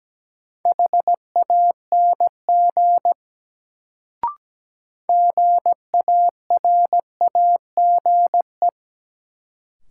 Morse code, heard as a keyed tone, about 17 wpm, 700 Hz. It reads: HANG GARAGE
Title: Sentences from Top 1600 Words - New Word Drill Alt - 17wpm